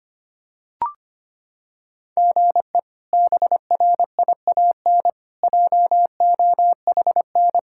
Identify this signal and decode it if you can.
Morse code, heard as a keyed tone, about 25 wpm, 700 Hz. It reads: GE BRIAN JOHN
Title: K1USN SST QSO Practice - 25wpm